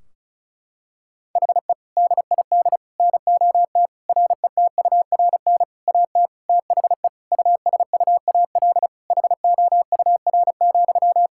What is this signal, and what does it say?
Morse code, heard as a keyed tone, about 36 wpm, 700 Hz. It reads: HE DID NOT RETURN AT THE USUAL HOUR,
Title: Sentences from Top 1600 Words 35wpm